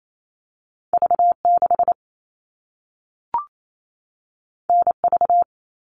Morse code, heard as a keyed tone, about 28 wpm, 700 Hz. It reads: V6 NV
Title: All Cut Numbers - 2-Digit 28wpm